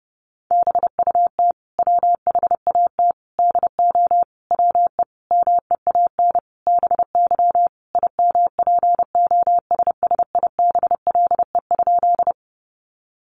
Morse code, heard as a keyed tone, about 30 wpm, 700 Hz. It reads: BUT WHAT DO WE MEAN BY IMPOSSIBLE?
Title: Sentences from Top 1100 Words 30wpm